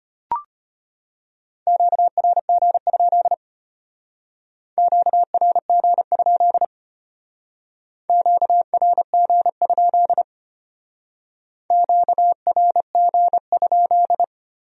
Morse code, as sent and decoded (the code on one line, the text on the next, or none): --.- .-. --. ..--.. / --.- .-. --. ..--.. / --.- .-. --. ..--.. / --.- .-. --. ..--..
QRG? QRG? QRG? QRG?